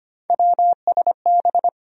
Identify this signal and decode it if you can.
Morse code, heard as a keyed tone, about 25 wpm, 700 Hz. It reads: WSB
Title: USA Call Sign Postfixes - 3 Letters 25wpm